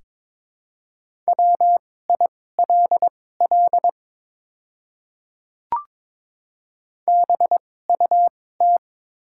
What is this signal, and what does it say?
Morse code, heard as a keyed tone, about 22 wpm, spaced as for 16 wpm, 700 Hz. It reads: WILL BUT
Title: 2x Character Spacing - Top 200 Words - 22wpm